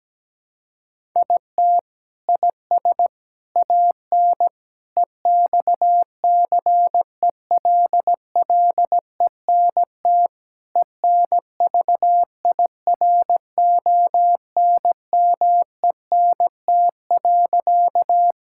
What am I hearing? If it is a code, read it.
Morse code, heard as a keyed tone, about 17 wpm, 700 Hz. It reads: IT IS AN EXCELLENT ENVIRONMENT.
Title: Sentences from Top 1000 Words - Review Part 5 - 17wpm